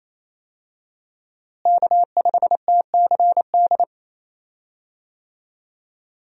Morse code, heard as a keyed tone, about 28 wpm, 700 Hz. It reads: K5TCD
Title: USA Call Signs - 5 Characters - Repeated 2x - 28wpm